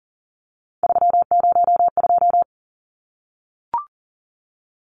Morse code, heard as a keyed tone, about 40 wpm, 700 Hz. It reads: 302
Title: Numbers - 3-Digit 40wpm